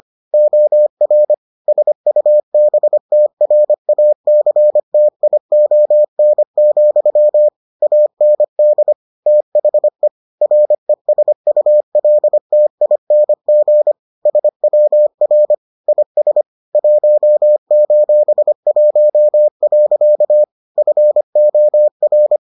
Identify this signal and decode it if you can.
Morse code, heard as a keyed tone, about 25 wpm, 600 Hz. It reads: OR SUBTRACTION, AND THE RESULTING SWR IS 1:1. FOR